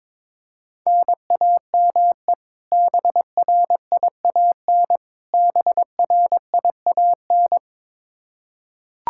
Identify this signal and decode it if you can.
Morse code, heard as a keyed tone, about 22 wpm, 700 Hz. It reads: NAME BRIAN BRIAN
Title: K1USN SST QSO Practice 22wpm